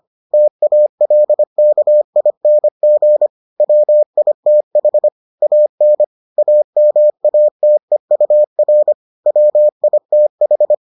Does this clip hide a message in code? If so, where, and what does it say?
Yes: Morse code from 0.3 to 10.7 s, heard as a keyed tone, about 25 wpm, 600 Hz. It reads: TALKING WITH AN AMATEUR WITH